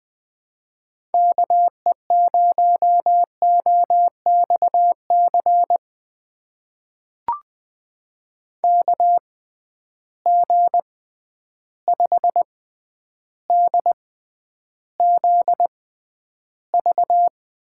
Morse code, heard as a keyed tone, about 20 wpm, spaced as for 10 wpm, 700 Hz. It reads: KE0OXC KG5DZV
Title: USA Call Signs - 6 Characters - 6x Character Spacing - 20wpm